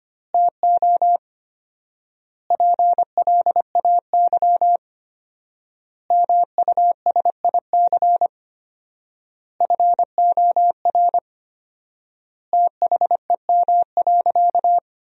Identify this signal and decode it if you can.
Morse code, heard as a keyed tone, about 25 wpm, 700 Hz. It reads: TO PLAY MUSIC FOR THEM.
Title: 4x Word-Spacing - Sentences from Top 300 Words - 25wpm